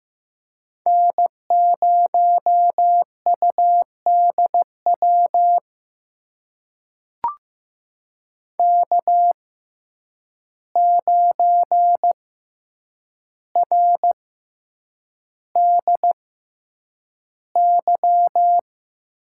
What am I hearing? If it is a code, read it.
Morse code, heard as a keyed tone, about 15 wpm, spaced as for 8 wpm, 700 Hz. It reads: N0UDW K9RDY
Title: USA Call Signs - 5 Characters - 6x Character Spacing - 15wpm